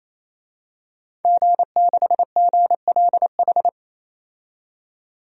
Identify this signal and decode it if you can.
Morse code, heard as a keyed tone, about 27 wpm, 700 Hz. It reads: G6GLH
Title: DXCC-100 Call Signs - 5 Characters - 28wpm